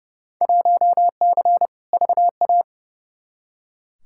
Morse code, heard as a keyed tone, about 30 wpm, 700 Hz. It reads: 1C VA
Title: Field Day Exchanges Encore - 30wpm